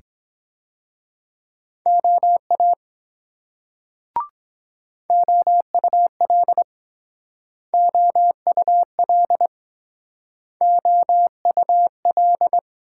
Morse code, heard as a keyed tone, about 22 wpm, 700 Hz. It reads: OA OUL OUL OUL